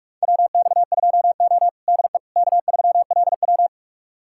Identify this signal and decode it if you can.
Morse code, heard as a keyed tone, about 45 wpm, 690 Hz. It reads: WX2O DE K3LW
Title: USA Callsign DE Callsign - 4 Characters - 45wpm